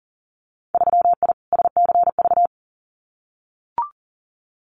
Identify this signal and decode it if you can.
Morse code, heard as a keyed tone, about 40 wpm, 700 Hz. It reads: 3I SCV
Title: Winter Field Day Exchanges Encore 40wpm